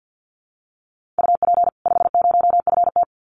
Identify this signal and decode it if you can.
Morse code, heard as a keyed tone, about 48 wpm, 700 Hz. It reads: UP 50FT
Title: Common QSO Phrases 50wpm